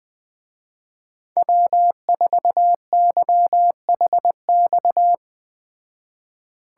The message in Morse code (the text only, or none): W4YHX